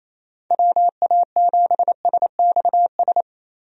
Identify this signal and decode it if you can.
Morse code, heard as a keyed tone, about 28 wpm, 700 Hz. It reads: WA7SXS
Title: USA Call Signs - 6 Characters - Repeated 3x - 28wpm